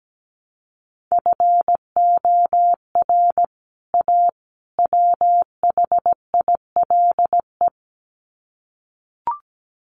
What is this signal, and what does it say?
Morse code, heard as a keyed tone, about 17 wpm, 700 Hz. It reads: FOR A WHILE
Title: Sets of 3 Words 17wpm